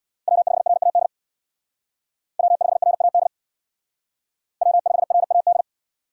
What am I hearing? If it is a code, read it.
Morse code, heard as a keyed tone, about 62 wpm, 700 Hz. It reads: W5RAD W5RAD W5RAD
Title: Speed Racing - USA 1x3 Call Signs 50wpm